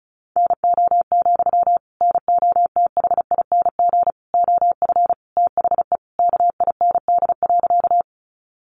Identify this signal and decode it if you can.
Morse code, heard as a keyed tone, about 34 wpm, 700 Hz. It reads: NO, NOTHING OF THE KIND.
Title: Sentences from Top 500 Words 35wpm